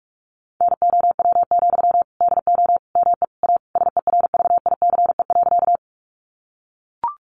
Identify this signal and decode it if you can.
Morse code, heard as a keyed tone, about 44 wpm, 700 Hz. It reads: NOW, DO ME A SERVICE.
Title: Sentences from Top 700 Words 45wpm